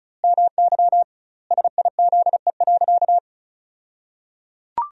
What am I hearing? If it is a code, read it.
Morse code, heard as a keyed tone, about 34 wpm, 700 Hz. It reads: MY SIZE.
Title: 2x Word-Spacing - Sentences from Top 400 Words - 35wpm